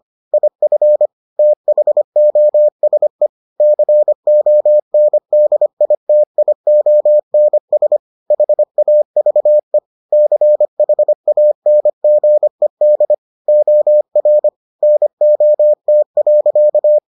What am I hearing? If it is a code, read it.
Morse code, heard as a keyed tone, about 25 wpm, 600 Hz. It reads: IF THOSE CONDITIONS HAVE CHANGED OR NOT.